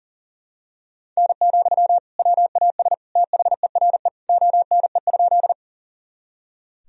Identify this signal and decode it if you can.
Morse code, heard as a keyed tone, about 40 wpm, 700 Hz. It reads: N, WAS THERE ONE?
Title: Sentences from Top 100 Words and Abbreviations 40wpm